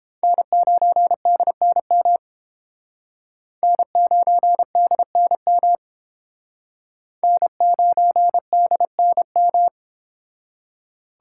Morse code, heard as a keyed tone, about 30 wpm, 700 Hz. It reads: N9DNM N9DNM N9DNM